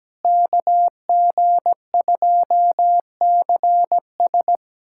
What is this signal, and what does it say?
Morse code, heard as a keyed tone, about 17 wpm, 700 Hz. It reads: KG2CS
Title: USA Callsign DE Callsign - 5 Characters - 17wpm